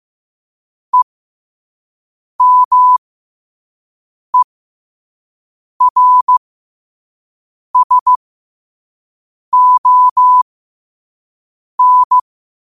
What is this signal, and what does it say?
Morse code, heard as a keyed tone, about 15 wpm, spaced as for 5 wpm, 1000 Hz. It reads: EMERSON
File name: qotdfiles / 5wpm